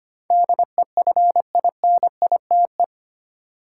Morse code, heard as a keyed tone, about 25 wpm, 700 Hz. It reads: DEFINITE
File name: Eight Letter Words - 25wpm